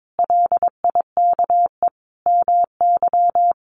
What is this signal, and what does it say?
Morse code, heard as a keyed tone, about 22 wpm, 700 Hz. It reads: LIKE MY
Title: Sentences from Top 1400 Words 22wpm